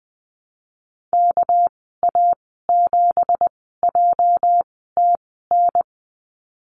Morse code, heard as a keyed tone, about 20 wpm, spaced as for 15 wpm, 700 Hz. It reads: KA7JTN
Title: 213 Lesson 34g - Number 8 - Call Signs 20wpm